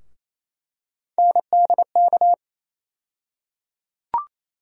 Morse code, heard as a keyed tone, about 28 wpm, 700 Hz. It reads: NDK
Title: USA Call Sign Postfixes - 3 Letters 28wpm